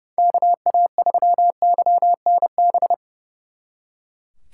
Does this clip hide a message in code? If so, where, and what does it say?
Yes: Morse code from 0.2 to 2.9 s, heard as a keyed tone, about 38 wpm, 700 Hz. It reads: KA3YNB